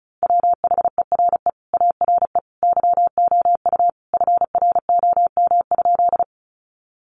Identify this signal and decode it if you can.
Morse code, heard as a keyed tone, about 34 wpm, 700 Hz. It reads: WHERE ARE YOU FROM?
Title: Sentences from Top 200 Words 35wpm